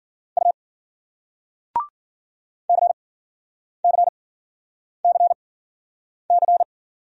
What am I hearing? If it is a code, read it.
Morse code, heard as a keyed tone, about 50 wpm, spaced as for 9 wpm, 700 Hz. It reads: U CCCC